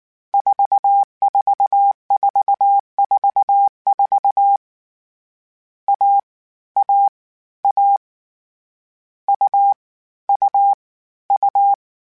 Morse code, heard as a keyed tone, about 19 wpm, 800 Hz. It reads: 44444 A A A U U U
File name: Difficult Combos / Difficult Combos-mp3s / AUV AUV4 @18